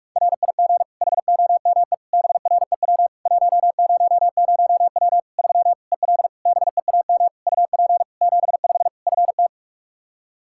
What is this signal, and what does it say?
Morse code, heard as a keyed tone, about 43 wpm, 700 Hz. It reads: RIG HOME BREW 100W 3 EL BEAM UP 75 FT
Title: K1USN SST QSO Practice 45wpm